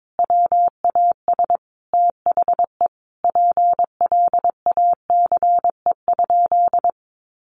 Morse code, heard as a keyed tone, about 22 wpm, 700 Hz. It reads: WAS THE PLACE?
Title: Sentences from Top 200 Words 22wpm